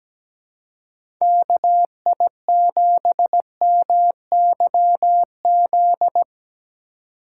Morse code, heard as a keyed tone, about 17 wpm, 700 Hz. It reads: KI7MYZ
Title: USA Call Signs - 6 Characters - 2x Character Spacing - 17wpm